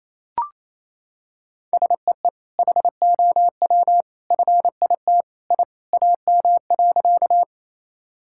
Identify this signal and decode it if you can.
Morse code, heard as a keyed tone, about 28 wpm, 700 Hz. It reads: SEE HOW FIT I AM.
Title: Sentences from Top 1000 Words - Review Part 5 - 28wpm